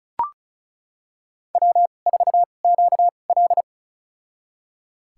Morse code, heard as a keyed tone, about 36 wpm, spaced as for 25 wpm, 690 Hz. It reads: W4QL
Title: USA Call Signs - 4 Characters - 2x Character Spacing - 35wpm